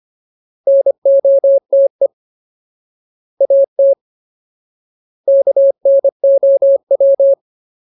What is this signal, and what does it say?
Morse code, heard as a keyed tone, about 25 wpm, 550 Hz. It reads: NOTE AT KNOW
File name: Ultimate Word List / 25WPM Ultimate